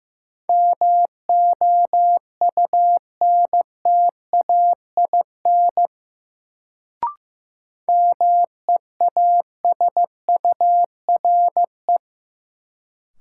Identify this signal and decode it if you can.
Morse code, heard as a keyed tone, about 15 wpm, 700 Hz. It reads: MOUNTAIN MEASURE
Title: Sentences from Top 1000 Words - Review Part 1 - New Word Drill - 15wpm